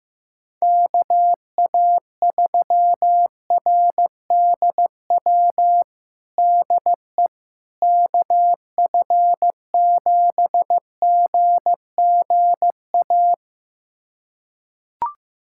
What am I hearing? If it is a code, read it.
Morse code, heard as a keyed tone, about 15 wpm, 700 Hz. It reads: KA3RDW DE KF7GGA
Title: USA Callsign DE Callsign 15wpm